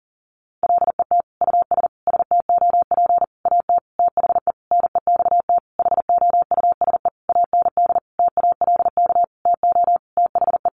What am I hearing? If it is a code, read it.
Morse code, heard as a keyed tone, about 40 wpm, 700 Hz. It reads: LET US STOP AT THE NEXT HOUSE AND TALK TO THE
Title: Sentences from Top 400 Words 40wpm